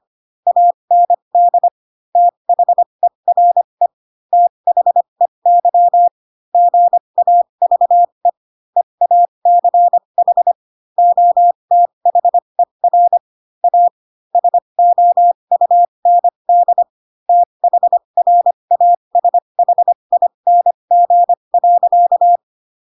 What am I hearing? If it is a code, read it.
Morse code, heard as a keyed tone, about 25 wpm, spaced as for 22 wpm, 700 Hz. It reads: AND THERE THEY GAVE EACH OTHER A SOUND THRASHING.
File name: Pinocchio / Pinocchio ch2 18 25 ik1hge